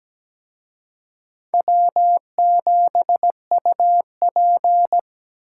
Morse code, heard as a keyed tone, about 20 wpm, 700 Hz. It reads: W7UP
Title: Speed Racing - USA 1x2 Call Signs 17wpm